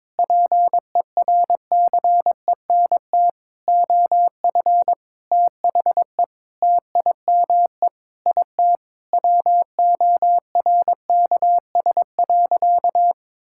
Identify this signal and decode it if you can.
Morse code, heard as a keyed tone, about 22 wpm, 700 Hz. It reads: PERCENT OF THE TIME IT WORKS.
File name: practice / 22wpm / Sentences with 2-Word Phrases - 22wpm